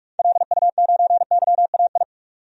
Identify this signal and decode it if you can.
Morse code, heard as a keyed tone, about 43 wpm, 700 Hz. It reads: LU9YAI
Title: DXCC-100 Call Signs - 6 Characters - 45wpm